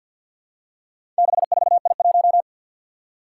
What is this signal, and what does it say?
Morse code, heard as a keyed tone, about 49 wpm, 700 Hz. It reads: 64I1
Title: Four Challenging Letter-Number Combinations 50wpm